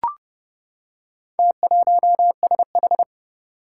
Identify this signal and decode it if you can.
Morse code, heard as a keyed tone, about 28 wpm, 700 Hz. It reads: T1SH
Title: Four Challenging Letter-Number Combinations 30wpm